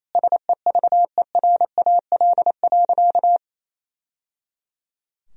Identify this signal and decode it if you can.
Morse code, heard as a keyed tone, about 27 wpm, 700 Hz. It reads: SEVERAL.
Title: Sentences from Top 1000 Words - Review Part 2 - 28wpm